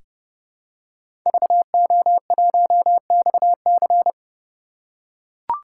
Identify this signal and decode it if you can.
Morse code, heard as a keyed tone, about 32 wpm, 700 Hz. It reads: VO1XC
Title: K1USN SST QSO Practice - 30wpm